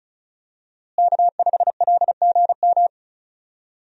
Morse code, heard as a keyed tone, about 34 wpm, 700 Hz. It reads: K5LGM
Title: USA Call Signs - 5 Characters - Repeated 2x - 35wpm